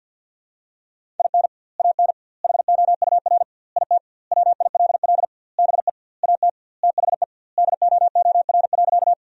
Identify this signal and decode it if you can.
Morse code, heard as a keyed tone, about 50 wpm, 700 Hz. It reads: IN AN HOUR IT WILL BE AT THE DOOR.
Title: 2x Word-Spacing - Sentences from Top 400 Words - 50wpm